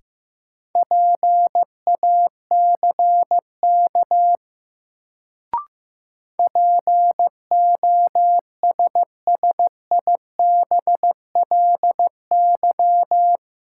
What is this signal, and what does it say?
Morse code, heard as a keyed tone, about 15 wpm, 700 Hz. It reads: PACK POSSIBLY
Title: Sentences from Top 1300 Words - New Word Drill - 15wpm